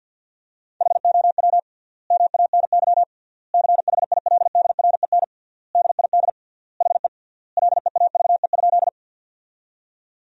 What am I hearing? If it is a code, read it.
Morse code, heard as a keyed tone, about 50 wpm, 700 Hz. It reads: HOW MANY CHILDREN DID HE LEAVE?
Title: 3x Word-Spacing - Sentences from Top 400 Words - 50wpm